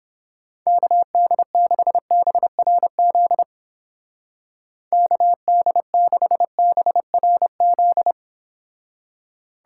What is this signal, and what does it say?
Morse code, heard as a keyed tone, about 31 wpm, 700 Hz. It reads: KD6BRZ KD6BRZ